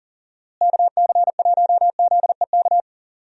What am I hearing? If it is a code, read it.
Morse code, heard as a keyed tone, about 40 wpm, 700 Hz. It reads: KC1ZEK
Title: USA Call Signs - 6 Characters - Repeated 3x - 40wpm